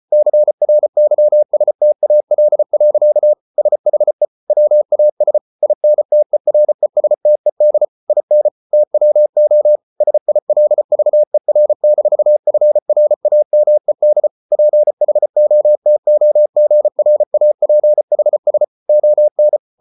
Morse code, heard as a keyed tone, about 34 wpm, 600 Hz. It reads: CRYSTAL. SHE WAS INTERESTED IN TWO SILVER-FRAMED PHOTOGRAPHS ON